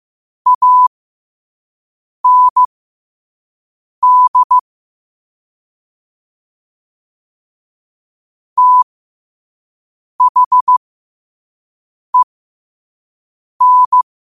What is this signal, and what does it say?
Morse code, heard as a keyed tone, about 15 wpm, spaced as for 5 wpm, 1000 Hz. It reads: AND THEN
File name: qotdfiles / 5wpm